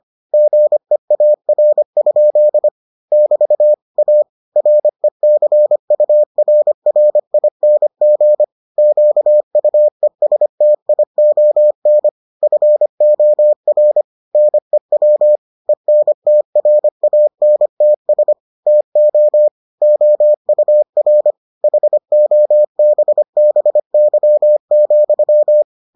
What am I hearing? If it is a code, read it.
Morse code, heard as a keyed tone, about 25 wpm, 600 Hz. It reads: GEAR? <BT> A RECURRING QUESTION FOR NEW ENTRANTS TO OUR HOBBY,